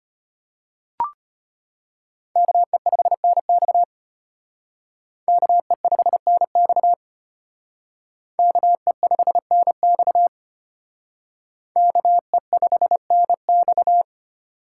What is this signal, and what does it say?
Morse code, heard as a keyed tone, about 32 wpm, 700 Hz. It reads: KE5NX KE5NX KE5NX KE5NX